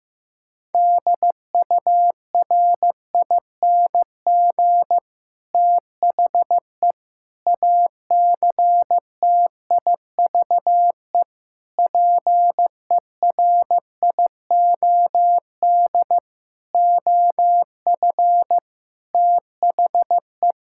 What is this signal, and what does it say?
Morse code, heard as a keyed tone, about 14 wpm, 700 Hz. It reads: DURING THE ACTIVE PERIOD OF THE
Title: Sentences from Top 1000 Words - Review Part 6 - 15wpm